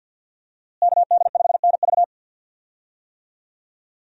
Morse code, heard as a keyed tone, about 46 wpm, 700 Hz. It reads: KD5NV
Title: USA Call Signs - 5 Characters - Repeated 3x - 50wpm